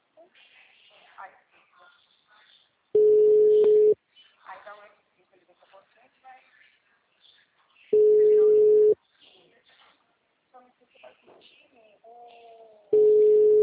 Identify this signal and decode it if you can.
Ringback tone (Europe, 425 Hz), with speech.
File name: out- -2013- - - .2560